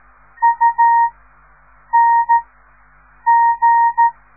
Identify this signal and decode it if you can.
Morse code, heard as a keyed tone, about 14 wpm, 930 Hz. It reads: U N G
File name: radiomorningstar / morse / les 4